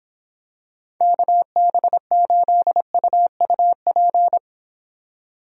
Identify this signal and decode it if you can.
Morse code, heard as a keyed tone, about 26 wpm, 700 Hz. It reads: KB8UUP